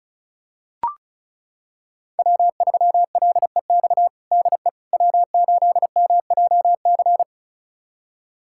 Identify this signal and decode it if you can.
Morse code, heard as a keyed tone, about 36 wpm, 700 Hz. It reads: W3LEX DE W8MJC
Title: USA Callsign DE Callsign - 5 Characters - 35wpm